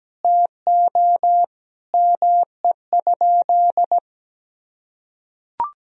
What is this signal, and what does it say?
Morse code, heard as a keyed tone, about 17 wpm, 700 Hz. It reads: TO ME?
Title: Sentences from Top 200 Words and Abbreviations 17wpm